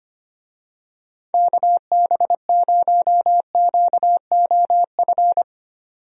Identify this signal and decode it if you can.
Morse code, heard as a keyed tone, about 25 wpm, 700 Hz. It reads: KB0QOF
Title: USA Call Signs - 6 Characters - 25wpm